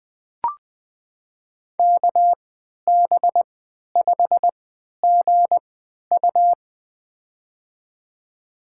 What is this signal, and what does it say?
Morse code, heard as a keyed tone, about 20 wpm, spaced as for 11 wpm, 700 Hz. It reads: KB5GU
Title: USA Call Signs - 5 Characters - 3x Character Spacing - 20wpm